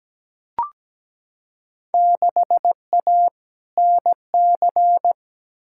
Morse code, heard as a keyed tone, about 17 wpm, 700 Hz. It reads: 6A NC
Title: Field Day Exchanges Encore - 17wpm